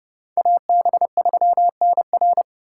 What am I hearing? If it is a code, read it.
Morse code, heard as a keyed tone, about 29 wpm, 700 Hz. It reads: AB3NR